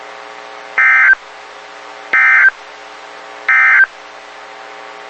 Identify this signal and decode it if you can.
Emergency Alert System (SAME) tones, heard as harsh buzzing bursts. It is heard from 0.7 to 3.9 s.
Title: WXK91 NOAA Weather Radio